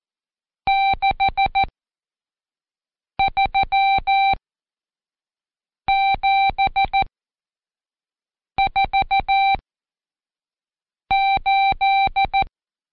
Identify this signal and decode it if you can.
Morse code, heard as a keyed tone, about 14 wpm, spaced as for 5 wpm, 780 Hz. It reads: 63748
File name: Code Quick / AudioDownld2 / Seg91-Day19-Review Numb